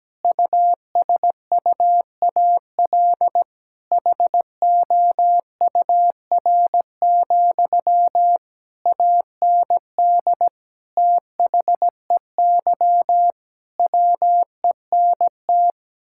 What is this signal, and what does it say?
Morse code, heard as a keyed tone, about 17 wpm, 700 Hz. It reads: USUAL HOUR, AND THEY WENT
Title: Sentences from Top 1600 Words 17wpm